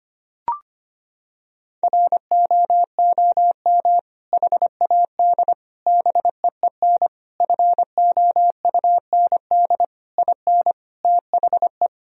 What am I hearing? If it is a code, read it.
Morse code, heard as a keyed tone, about 25 wpm, 700 Hz. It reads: ROOM HAD BEEN FOUND IN THE
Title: Sentences from Top 1000 Words - Review Part 3 - 25wpm